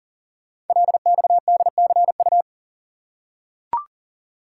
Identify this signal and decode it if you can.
Morse code, heard as a keyed tone, about 42 wpm, 690 Hz. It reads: LXDCU
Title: Code Groups - 5 Letters - 40wpm